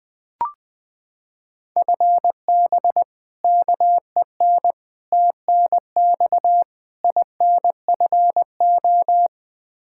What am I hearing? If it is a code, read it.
Morse code, heard as a keyed tone, about 20 wpm, 700 Hz. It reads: FB KEN TNX INFO
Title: Practice QSO - Exchange 3 - More Info - 20wpm